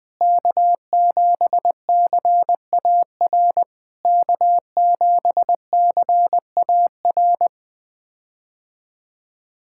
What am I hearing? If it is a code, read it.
Morse code, heard as a keyed tone, about 20 wpm, 700 Hz. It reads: K7CAR K7CAR
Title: POTA QSO Practice - 20wpm